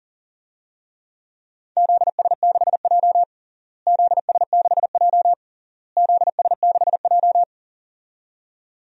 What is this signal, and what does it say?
Morse code, heard as a keyed tone, about 40 wpm, 700 Hz. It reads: ZS6J ZS6J ZS6J
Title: DXCC-100 Call Signs - 4 Characters - Repeated 3x - 40wpm